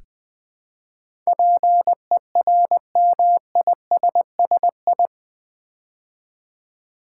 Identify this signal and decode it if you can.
Morse code, heard as a keyed tone, about 21 wpm, 700 Hz. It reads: PERMISSI
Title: Build A Word - 4 Letters At A Time - 10-Letter Words - 20wpm